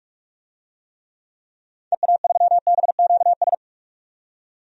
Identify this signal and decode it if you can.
Morse code, heard as a keyed tone, about 46 wpm, 700 Hz. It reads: EA3BQS
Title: DXCC-100 Call Signs - 6 Characters - 45wpm